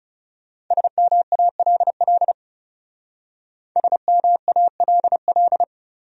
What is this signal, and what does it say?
Morse code, heard as a keyed tone, about 36 wpm, 700 Hz. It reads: SMALL SMALL
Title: Speed-Racing Top 500 Words 30wpm